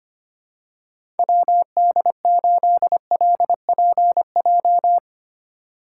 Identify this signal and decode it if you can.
Morse code, heard as a keyed tone, about 25 wpm, 700 Hz. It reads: WD8LPJ
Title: USA Call Signs - 6 Characters - 3x Character Spacing - 25wpm